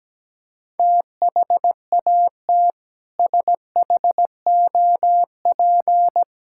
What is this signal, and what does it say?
Morse code, heard as a keyed tone, about 17 wpm, 700 Hz. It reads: THAT SHOP
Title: Sentences from Top 1600 Words 17wpm